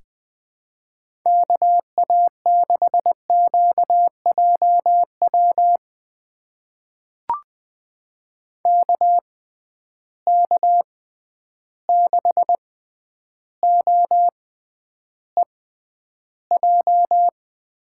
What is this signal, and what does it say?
Morse code, heard as a keyed tone, about 20 wpm, spaced as for 10 wpm, 700 Hz. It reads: KA6QJW KK6OEJ
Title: USA Call Signs - 6 Characters - 6x Character Spacing - 20wpm